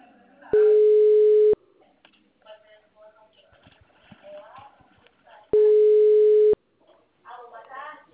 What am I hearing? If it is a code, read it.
Ringback tone (Europe, 425 Hz), with speech.
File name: out- -2020- - - .11